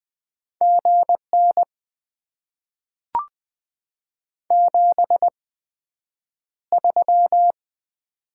Morse code, heard as a keyed tone, about 20 wpm, spaced as for 9 wpm, 700 Hz. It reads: GN 73
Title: Top 90 QSO Elements - 8x Character Spacing - 20wpm